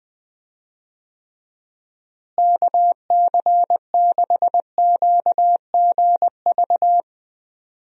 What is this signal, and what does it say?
Morse code, heard as a keyed tone, about 20 wpm, 700 Hz. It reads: KC6QGV
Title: USA Call Signs - 6 Characters - Repeated 3x - 20wpm